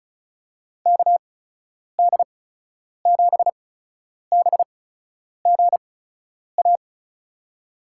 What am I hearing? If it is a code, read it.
Morse code, heard as a keyed tone, about 36 wpm, spaced as for 10 wpm, 700 Hz. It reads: KD7BGA
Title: USA Call Signs - 6 Characters - 8x Character Spacing - 35wpm